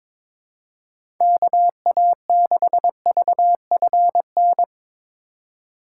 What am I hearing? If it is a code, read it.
Morse code, heard as a keyed tone, about 24 wpm, 700 Hz. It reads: KA6VFN
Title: Speed Racing - USA 2x3 Call Signs 22wpm